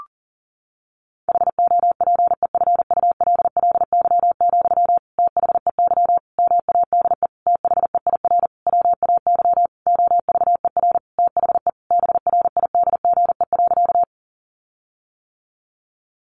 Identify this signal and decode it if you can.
Morse code, heard as a keyed tone, about 40 wpm, 700 Hz. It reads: HOPEFULLY, THEY MADE THEIR WAY OVER THE BRIDGE.
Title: Sentences from Top 1500 Words 40wpm